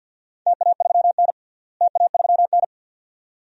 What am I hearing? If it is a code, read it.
Morse code, heard as a keyed tone, about 52 wpm, 700 Hz. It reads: TA3N TA3N
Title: DXCC-100 Call Signs - 4 Characters - Repeated 2x - 50wpm